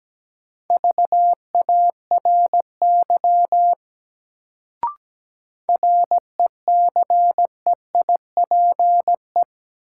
Morse code, heard as a keyed tone, about 17 wpm, 700 Hz. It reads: VARY RECEIPE
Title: Sentences from Top 1200 Words - New Word Drill Alt - 17wpm